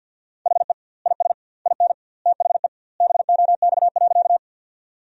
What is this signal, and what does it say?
Morse code, heard as a keyed tone, about 47 wpm, 700 Hz. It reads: HE IS IN THE BOX.
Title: 2x Word-Spacing - Sentences from Top 500 Words - 50wpm